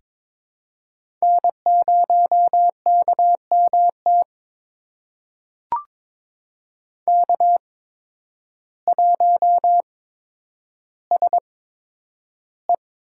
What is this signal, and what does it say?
Morse code, heard as a keyed tone, about 21 wpm, spaced as for 10 wpm, 700 Hz. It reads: N0KMT K1SE
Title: USA Call Signs - 5 Characters - 8x Character Spacing - 22wpm